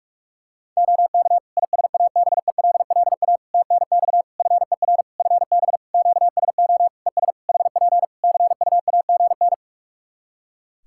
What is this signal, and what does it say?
Morse code, heard as a keyed tone, about 46 wpm, 700 Hz. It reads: OK ISABELLA TNX FER FB QSO ES HP CUAGN
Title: Practice QSO - Exchange 4 - Ending - 45wpm